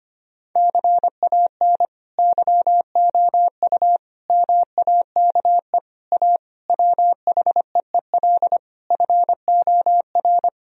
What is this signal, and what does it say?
Morse code, heard as a keyed tone, about 25 wpm, 700 Hz. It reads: CAN YOU MAKE A WHEEL FOR
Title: Sentences from Top 1000 Words - Review Part 3 - 25wpm